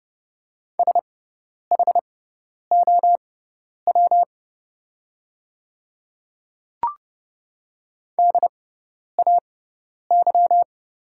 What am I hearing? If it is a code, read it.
Morse code, heard as a keyed tone, about 30 wpm, spaced as for 10 wpm, 700 Hz. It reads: SHOW DAY
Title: 6x Character Spacing - Top 300 Words - 30wpm